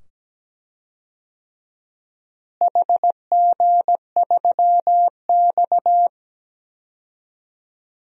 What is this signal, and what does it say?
Morse code, heard as a keyed tone, about 17 wpm, 700 Hz. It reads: HG3X
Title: DXCC-100 Call Signs - 4 Characters - Repeated 2x - 17wpm